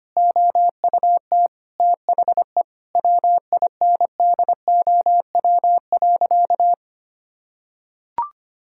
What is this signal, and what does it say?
Morse code, heard as a keyed tone, about 25 wpm, 700 Hz. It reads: OUT THE WINDOW.
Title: Sentences from Top 1600 Words 25wpm